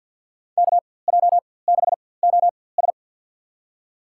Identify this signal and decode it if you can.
Morse code, heard as a keyed tone, about 47 wpm, spaced as for 23 wpm, 700 Hz. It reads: KJ6OS
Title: USA Call Signs - 5 Characters - 4x Character Spacing - 50wpm